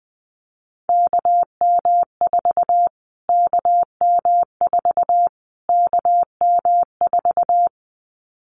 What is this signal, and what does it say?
Morse code, heard as a keyed tone, about 20 wpm, 700 Hz. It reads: KM4 KM4 KM4
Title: Callsign Prefixes for Top 25 DXCC - Repeated 3x - 20wpm